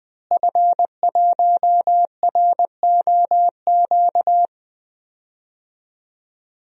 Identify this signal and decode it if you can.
Morse code, heard as a keyed tone, about 20 wpm, 700 Hz. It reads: F1ROQ
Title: DXCC-100 Call Signs - 5 Characters - Repeated 3x - 20wpm